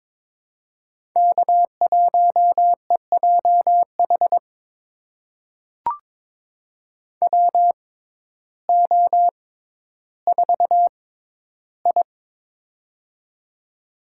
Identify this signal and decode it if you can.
Morse code, heard as a keyed tone, about 22 wpm, spaced as for 12 wpm, 700 Hz. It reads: K1EJH WO4I
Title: USA Call Signs - 6x Character Spacing - 22wpm